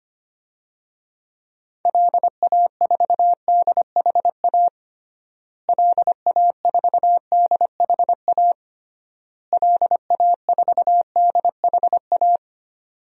Morse code, heard as a keyed tone, about 25 wpm, 700 Hz. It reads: LA4DHA LA4DHA LA4DHA
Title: DXCC-100 Call Signs - 6 Characters - Repeated 3x - 25wpm